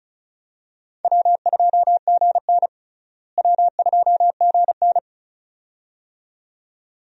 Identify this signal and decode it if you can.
Morse code, heard as a keyed tone, about 36 wpm, 700 Hz. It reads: W2GN W2GN
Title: USA Call Signs - 4 Characters - Repeated 2x - 35wpm